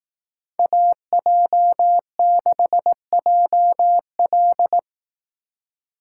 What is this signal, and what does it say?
Morse code, heard as a keyed tone, about 18 wpm, 700 Hz. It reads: AJ6JL
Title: Speed Racing - USA 2x2 Call Signs 15wpm